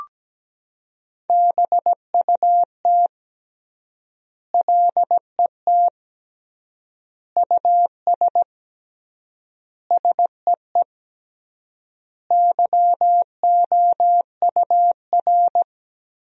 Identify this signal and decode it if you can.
Morse code, heard as a keyed tone, about 17 wpm, 700 Hz. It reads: BUT LET US SEE YOUR
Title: 3x Word-Spacing - Sentences from Top 500 Words - 17wpm